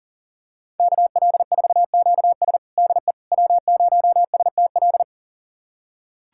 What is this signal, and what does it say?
Morse code, heard as a keyed tone, about 40 wpm, 700 Hz. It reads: KL4QS DE W0STL
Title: USA Callsign DE Callsign - 5 Characters - 40wpm